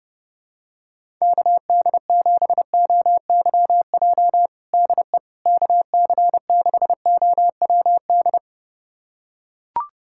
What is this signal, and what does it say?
Morse code, heard as a keyed tone, about 30 wpm, 700 Hz. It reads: KD7OYJ DE KC6OWD
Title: USA Callsign DE Callsign - 6 Characters - 30wpm